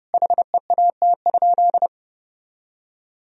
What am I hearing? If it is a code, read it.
Morse code, heard as a keyed tone, about 30 wpm, 700 Hz. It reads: HEAT?
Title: Sentences from Top 1000 Words - Review Part 3 - 30wpm